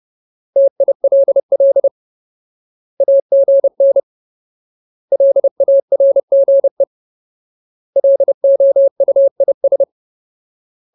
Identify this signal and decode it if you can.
Morse code, heard as a keyed tone, about 30 wpm, 550 Hz. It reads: TILL AGN LARGE LOUIS
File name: Ultimate Word List / 30WPM Ultimate